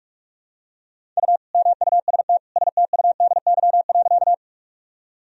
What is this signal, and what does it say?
Morse code, heard as a keyed tone, about 46 wpm, 700 Hz. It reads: U MUST STUDY.
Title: Sentences from Top 300 Words and Abbreviations 45wpm